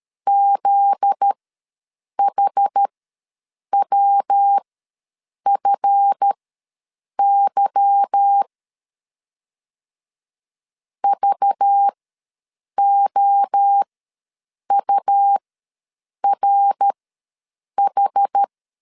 Morse code, heard as a keyed tone, about 13 wpm, spaced as for 7 wpm, 800 Hz. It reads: ZHWFY VOURH
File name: LEZIONI / 9 28 ws 12wpm 3354